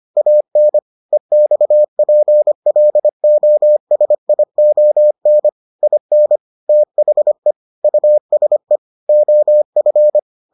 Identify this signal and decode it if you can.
Morse code, heard as a keyed tone, about 25 wpm, 600 Hz. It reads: AN EXPLOSION IN THE USE OF